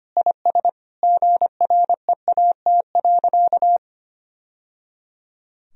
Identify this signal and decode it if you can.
Morse code, heard as a keyed tone, about 25 wpm, 700 Hz. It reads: IS GREAT.
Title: Sentences from Top 1000 Words - Review Part 1 - 25wpm